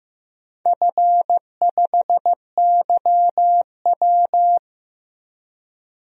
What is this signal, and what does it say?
Morse code, heard as a keyed tone, about 15 wpm, 700 Hz. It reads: F5YW
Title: DXCC-100 Call Signs - 4 Characters - 15wpm